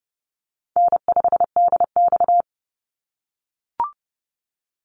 Morse code, heard as a keyed tone, about 30 wpm, 700 Hz. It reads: N5DX
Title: Easy Call Signs - 30wpm